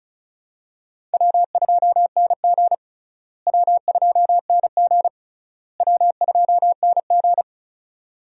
Morse code, heard as a keyed tone, about 34 wpm, 700 Hz. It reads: W2NG W2NG W2NG
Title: USA Call Signs - 4 Characters - Repeated 3x - 35wpm